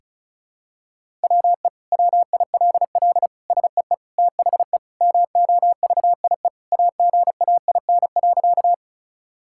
Morse code, heard as a keyed tone, about 36 wpm, 700 Hz. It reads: WE WILL SEE THE MOVIE AGAIN.
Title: Sentences from Top 1000 Words - Review Part 5 - 35wpm